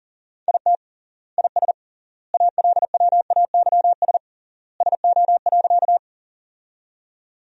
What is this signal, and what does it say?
Morse code, heard as a keyed tone, about 40 wpm, 700 Hz. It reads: IT IS ALWAYS SO.
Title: 3x Word-Spacing - Sentences from Top 300 Words - 40wpm